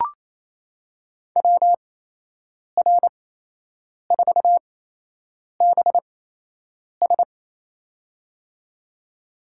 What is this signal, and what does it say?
Morse code, heard as a keyed tone, about 28 wpm, spaced as for 8 wpm, 700 Hz. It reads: WR4BS
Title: USA Call Signs - 5 Characters - 8x Character Spacing - 28wpm